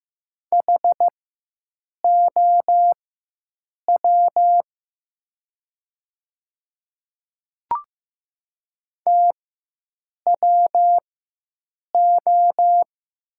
Morse code, heard as a keyed tone, about 15 wpm, spaced as for 7 wpm, 700 Hz. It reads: HOW TWO
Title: 4x Character Spacing - Top 100 Words - 15wpm